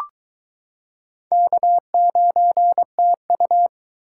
Morse code, heard as a keyed tone, about 22 wpm, 700 Hz. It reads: K9TU